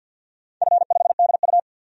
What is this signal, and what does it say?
Morse code, heard as a keyed tone, about 53 wpm, 700 Hz. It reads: F5DU
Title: DXCC-100 Call Signs - 4 Characters - 50wpm